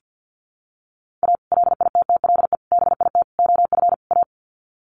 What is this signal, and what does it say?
Morse code, heard as a keyed tone, about 50 wpm, 700 Hz. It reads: A LITTLE BIT OF A
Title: Sets of 5 Words 50wpm